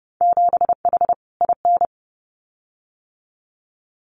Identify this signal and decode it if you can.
Morse code, heard as a keyed tone, about 30 wpm, 700 Hz. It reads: 7H IN
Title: Winter Field Day Exchanges Encore 30wpm